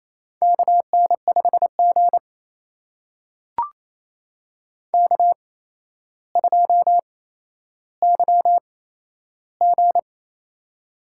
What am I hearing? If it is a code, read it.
Morse code, heard as a keyed tone, about 27 wpm, spaced as for 12 wpm, 700 Hz. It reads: KN5G K2YG
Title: USA Call Signs - 4 Characters - 8x Character Spacing - 28wpm